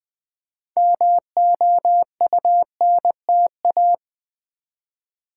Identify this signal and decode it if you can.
Morse code, heard as a keyed tone, about 20 wpm, 690 Hz. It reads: MOUNTA
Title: Build A Word - 3 Letters At A Time - 8-Letter Words - 20wpm